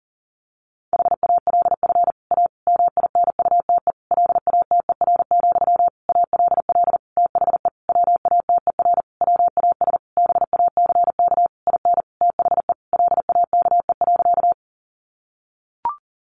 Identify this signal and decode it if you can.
Morse code, heard as a keyed tone, about 40 wpm, 700 Hz. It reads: HALF A MINUTE LATER, ALL THE WATER WAS BACK IN THE LAKE.
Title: Sentences from Top 1100 Words 40wpm